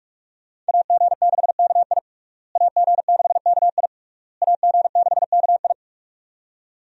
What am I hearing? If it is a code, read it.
Morse code, heard as a keyed tone, about 41 wpm, 700 Hz. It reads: AG6KI AG6KI AG6KI
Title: USA Call Signs - 5 Characters - Repeated 3x - 45wpm